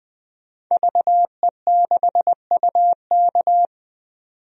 Morse code, heard as a keyed tone, about 20 wpm, 700 Hz. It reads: VE6UK
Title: DXCC-100 Call Signs - 5 Characters - 20wpm